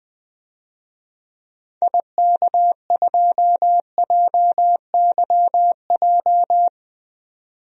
Morse code, heard as a keyed tone, about 20 wpm, 700 Hz. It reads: IK2JYJ
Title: DXCC-100 Call Signs - 6 Characters - 20wpm